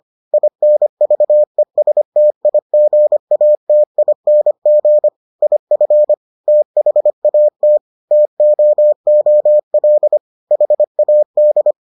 Morse code, heard as a keyed tone, about 25 wpm, 600 Hz. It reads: INVESTIGATING IF THAT TOOL HAD